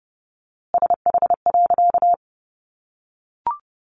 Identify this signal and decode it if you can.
Morse code, heard as a keyed tone, about 30 wpm, 700 Hz. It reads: SH.
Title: A Sound Of Thunder 30wpm